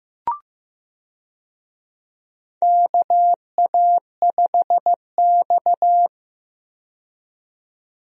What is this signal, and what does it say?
Morse code, heard as a keyed tone, about 15 wpm, 700 Hz. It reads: KA5X
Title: USA Call Signs - 4 Characters - 15wpm